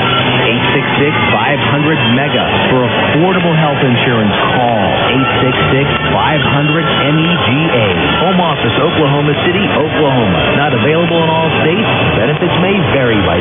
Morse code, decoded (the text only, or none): none